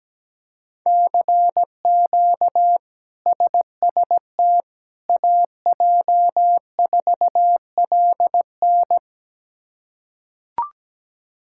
Morse code, heard as a keyed tone, about 17 wpm, 700 Hz. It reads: CQ SST AJ4LN
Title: K1USN SST QSO Practice - 17wpm